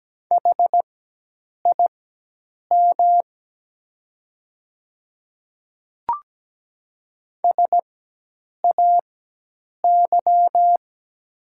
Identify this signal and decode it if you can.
Morse code, heard as a keyed tone, about 17 wpm, spaced as for 8 wpm, 700 Hz. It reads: HIM SAY
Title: 4x Character Spacing - Top 200 Words - 17wpm